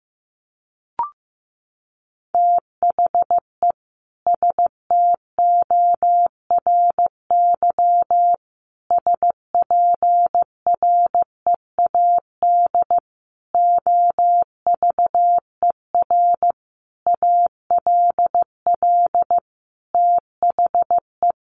Morse code, heard as a keyed tone, about 15 wpm, 700 Hz. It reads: THE STORY SPREAD OVER ALL THE
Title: Sentences from Top 1200 Words 15wpm